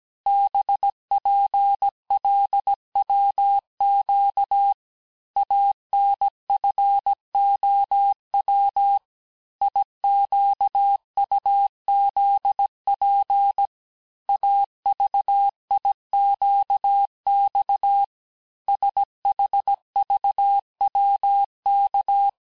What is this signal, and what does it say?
Morse code, heard as a keyed tone, about 17 wpm, 800 Hz. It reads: BPLWQ ANFOW IQUZP AVIQX SHVWK
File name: cours-du-lundi files / corrige-cours-1581-texte-chiffre-vitesse-16-mpm